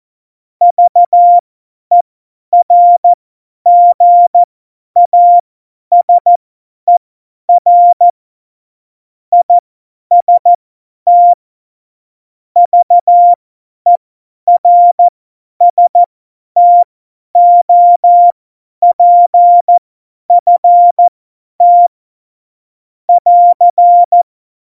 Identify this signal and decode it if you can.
Morse code, heard as a keyed tone, about 14 wpm, spaced as for 10 wpm, 700 Hz. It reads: VERGASER IST VERSTOPFT <AR>
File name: Morsekurs / Witze / Witz 23. Januar 2018, 50 F70